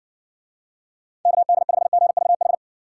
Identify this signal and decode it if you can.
Morse code, heard as a keyed tone, about 60 wpm, 700 Hz. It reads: KD5GVH